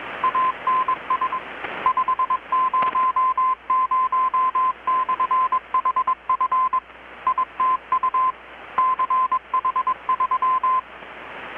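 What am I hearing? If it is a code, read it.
Morse code, heard as a keyed tone, about 21 wpm, 1030 Hz. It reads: ANS 500/HF ITU CH3
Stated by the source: Morse from KPH's Night of Nights XX maritime radiotelegraphy event, received in Chiapas, Mexico about 3,800 km away on a Kenwood R-600 with a grounded 10 m long-wire antenna.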